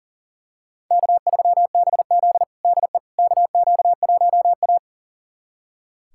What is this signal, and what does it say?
Morse code, heard as a keyed tone, about 40 wpm, 700 Hz. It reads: K3BZ DE KQ1A
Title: USA Callsign DE Callsign - 4 Characters - 40wpm